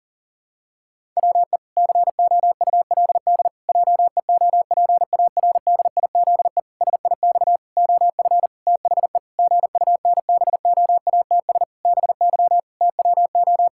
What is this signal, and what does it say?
Morse code, heard as a keyed tone, about 40 wpm, 700 Hz. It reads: WE COULD JEOPARDIZE SIX OF THE GUNBOATS BY TWO
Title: Pangram Challenge v2 - 40wpm